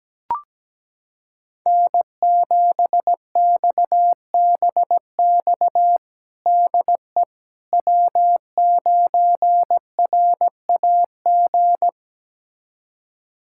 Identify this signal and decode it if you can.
Morse code, heard as a keyed tone, about 17 wpm, 700 Hz. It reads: N7XBX DE W9RAG
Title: USA Callsign DE Callsign - 5 Characters - 17wpm